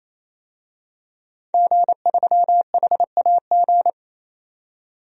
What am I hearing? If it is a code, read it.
Morse code, heard as a keyed tone, about 28 wpm, 700 Hz. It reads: G3HAG
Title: DXCC-100 Call Signs - 5 Characters - 28wpm